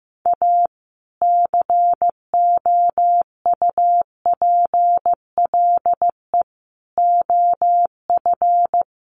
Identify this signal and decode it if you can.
Morse code, heard as a keyed tone, about 15 wpm, 700 Hz. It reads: A COUPLE OF
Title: Sets of 4 Words 15wpm